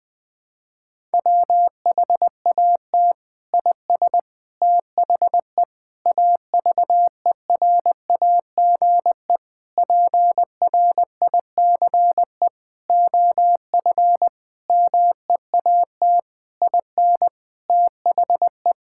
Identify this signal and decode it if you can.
Morse code, heard as a keyed tone, about 20 wpm, 700 Hz. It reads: WHAT IS THE AVERAGE PRICE OF MEAT IN THE
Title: Sentences from Top 1000 Words - Review Part 6 - 20wpm